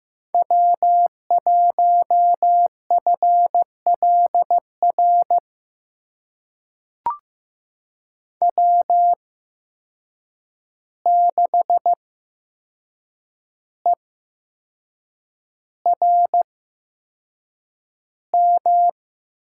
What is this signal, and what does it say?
Morse code, heard as a keyed tone, about 15 wpm, spaced as for 6 wpm, 700 Hz. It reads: W1FLR W6ERM
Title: USA Call Signs - 5 Characters - 8x Character Spacing - 15wpm